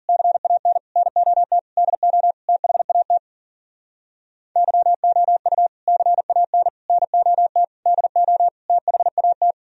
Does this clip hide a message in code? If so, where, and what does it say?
Yes: Morse code from 0.1 to 9.5 s, heard as a keyed tone, about 52 wpm, 700 Hz. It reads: CAN NOT DO THAT YOU CAN NOT DO THAT